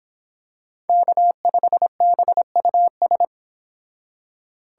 Morse code, heard as a keyed tone, about 25 wpm, 700 Hz. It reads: K5BUS